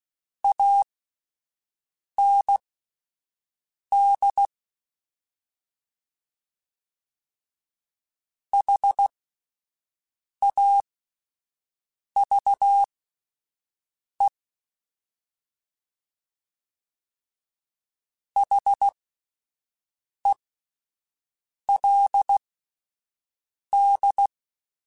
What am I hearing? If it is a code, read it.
Morse code, heard as a keyed tone, about 16 wpm, spaced as for 5 wpm, 800 Hz. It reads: AND HAVE HELD
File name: (spaced) / QSOs Spaced-mp3s / WA7QOX 2 @10 sp